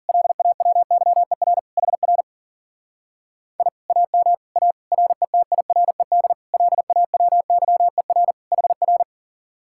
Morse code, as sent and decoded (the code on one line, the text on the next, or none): .-.. .- .-- -.-- . .-. / .... .-. / .. / .- -- / .- / .-. - .. .-. -.. / .-.. .- .-- -.-- . .-. / .... .-.
LAWYER HR I AM A RETIRED LAWYER HR